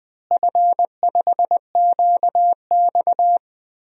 Morse code, heard as a keyed tone, about 20 wpm, 700 Hz. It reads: F5QX
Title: DXCC-100 Call Signs - 4 Characters - Repeated 2x - 20wpm